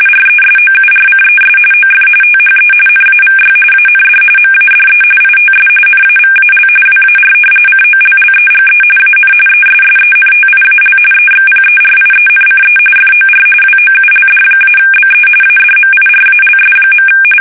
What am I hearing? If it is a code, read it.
RTTY, about 45.45 baud. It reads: RESOURCES OF THE NAVY AND MADE ITS NAVAL WAR COLLEGE AN ESSENTIAL EXPERIENCE FOR WOULD-BE ADMIRALS.
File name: fsk850